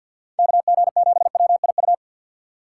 Morse code, heard as a keyed tone, about 51 wpm, 700 Hz. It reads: KG7WIU
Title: USA Call Signs - 6 Characters - 4x Character Spacing - 50wpm